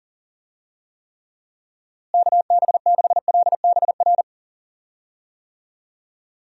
Morse code, heard as a keyed tone, about 40 wpm, 700 Hz. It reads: KB6LBR
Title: USA Call Signs - 6 Characters - Repeated 2x - 40wpm